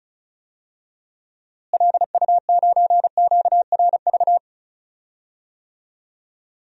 Morse code, heard as a keyed tone, about 36 wpm, 700 Hz. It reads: LU9QRV
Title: DXCC-100 Call Signs - 6 Characters - Repeated 2x - 35wpm